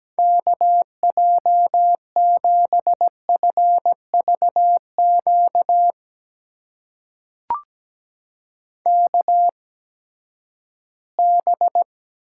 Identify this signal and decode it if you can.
Morse code, heard as a keyed tone, about 17 wpm, spaced as for 12 wpm, 700 Hz. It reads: KJ7FVQ KB
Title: USA Call Signs - 6 Characters - 8x Character Spacing - 17wpm